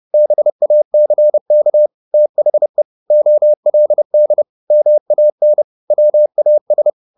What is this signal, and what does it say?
Morse code, heard as a keyed tone, about 30 wpm, 600 Hz. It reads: BACK THE OLD MAN WAS
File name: the old man and the sea / the old man and the sea 1 30wpm